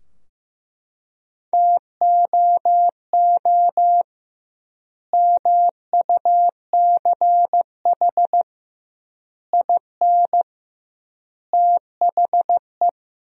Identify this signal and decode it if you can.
Morse code, heard as a keyed tone, about 16 wpm, 700 Hz. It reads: TOO MUCH IN THE
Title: 2x Word-Spacing - Sentences from Top 200 Words - 15wpm